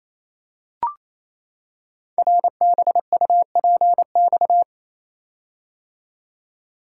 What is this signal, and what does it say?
Morse code, heard as a keyed tone, about 27 wpm, 690 Hz. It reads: RBUPX
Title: Code Groups - 5 Letters - 28wpm